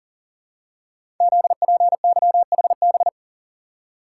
Morse code, heard as a keyed tone, about 42 wpm, 690 Hz. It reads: ZPYHB
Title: Code Groups - 5 Letter:Numbers - 40wpm